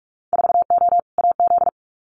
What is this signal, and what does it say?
Morse code, heard as a keyed tone, about 46 wpm, 700 Hz. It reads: 4O AZ
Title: Winter Field Day Exchanges Encore 45wpm